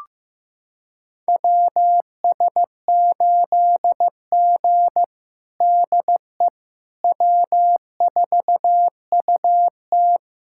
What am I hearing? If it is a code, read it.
Morse code, heard as a keyed tone, about 15 wpm, 700 Hz. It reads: WS8G DE W4UT
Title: USA Callsign DE Callsign - 4 Characters - 15wpm